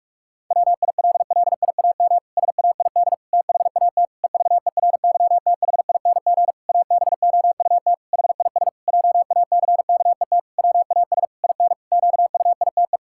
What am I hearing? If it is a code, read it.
Morse code, heard as a keyed tone, about 46 wpm, 700 Hz. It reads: WILLIAM SAID THAT EVERYTHING ABOUT HIS JACKET WAS IN QUITE
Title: Pangram Challenge v2 - 45wpm